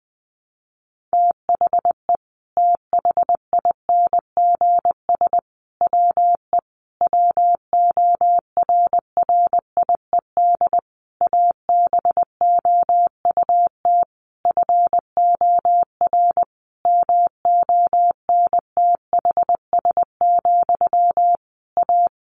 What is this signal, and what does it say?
Morse code, heard as a keyed tone, about 20 wpm, 700 Hz. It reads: THE THINGS WE WORRIED ABOUT FOR MONTHS, A
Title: A Sound Of Thunder 20wpm